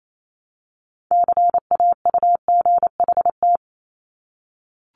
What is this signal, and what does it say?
Morse code, heard as a keyed tone, about 28 wpm, 700 Hz. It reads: CAUGHT
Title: Multi-letter Phonemes 28wpm